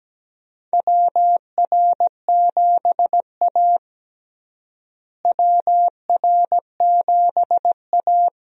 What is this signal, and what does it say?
Morse code, heard as a keyed tone, about 17 wpm, 700 Hz. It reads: WR7A WR7A
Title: USA Call Signs - 4 Characters - Repeated 2x - 17wpm